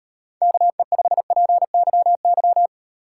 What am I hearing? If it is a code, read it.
Morse code, heard as a keyed tone, about 38 wpm, 700 Hz. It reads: KE5PYY